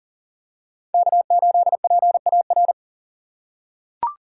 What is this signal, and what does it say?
Morse code, heard as a keyed tone, about 37 wpm, 700 Hz. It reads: K8PAR
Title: USA Call Signs - 5 Characters - 4x Character Spacing - 40wpm